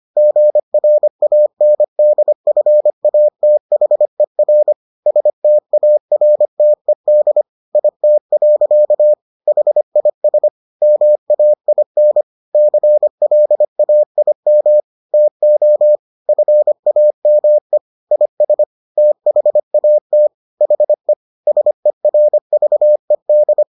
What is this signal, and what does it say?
Morse code, heard as a keyed tone, about 25 wpm, 600 Hz. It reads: GRANDFATHER STARTED IT. HIS MAIN CLAIM TO FAME IS THAT HE SERVED